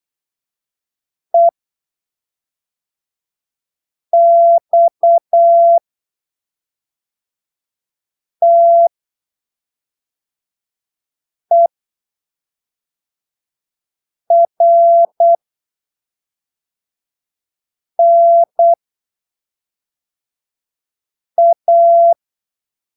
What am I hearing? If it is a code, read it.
Morse code, heard as a keyed tone, about 8 wpm, spaced as for 3 wpm, 680 Hz. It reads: EXTERNA
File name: arquivos / texto / texto 11 08ppm